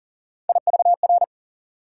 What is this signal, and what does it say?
Morse code, heard as a keyed tone, about 40 wpm, 700 Hz. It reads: IVR
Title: USA Call Sign Postfixes - 3 Letters 40wpm